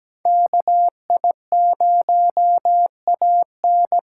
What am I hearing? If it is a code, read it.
Morse code, heard as a keyed tone, about 17 wpm, 700 Hz. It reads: KI0AN
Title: USA Call Signs - 2x Character Spacing - 17wpm